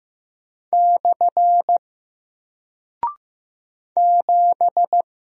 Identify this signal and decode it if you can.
Morse code, heard as a keyed tone, about 15 wpm, 700 Hz. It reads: / 7
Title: Single Character (Letter:Number:Punctuation:Prosign) Alt 15wpm